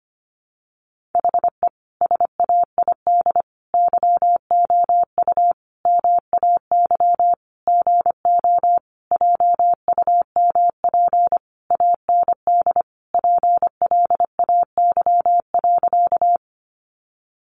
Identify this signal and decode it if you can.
Morse code, heard as a keyed tone, about 25 wpm, 700 Hz. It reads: HE SAID YOU MAY GO JUMP AND PLAY.
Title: Sentences from Top 1300 Words 25wpm